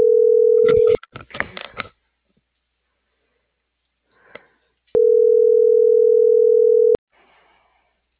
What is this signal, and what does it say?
Ringback tone (North America), 440 + 480 Hz.